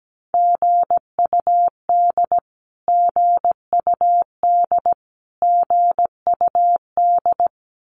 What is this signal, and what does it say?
Morse code, heard as a keyed tone, about 17 wpm, 700 Hz. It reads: GUD GUD GUD
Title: Abbreviations Repeated 3x - 17wpm